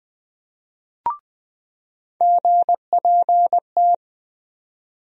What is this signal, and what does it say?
Morse code, heard as a keyed tone, about 21 wpm, 700 Hz. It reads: GPT
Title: USA Call Sign Postfixes - 3 Letters 20wpm